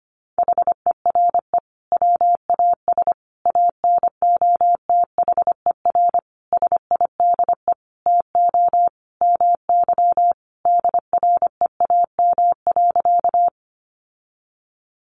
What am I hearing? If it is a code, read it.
Morse code, heard as a keyed tone, about 25 wpm, 700 Hz. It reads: HERE WAS ANOTHER SIDE TO MY DREAM.
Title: Sentences from Top 1500 Words 25wpm